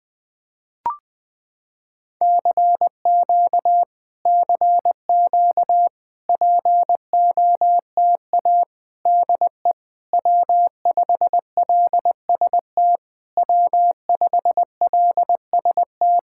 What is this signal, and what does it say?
Morse code, heard as a keyed tone, about 20 wpm, 700 Hz. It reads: CQ CQ POTA DE W5LST W5LST
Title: POTA QSO Practice - 20wpm